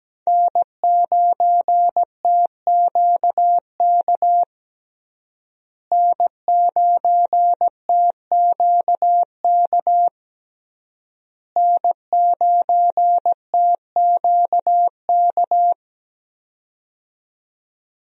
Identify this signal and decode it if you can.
Morse code, heard as a keyed tone, about 17 wpm, 700 Hz. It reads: N9TQK N9TQK N9TQK
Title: USA Call Signs - 5 Characters - Repeated 3x - 17wpm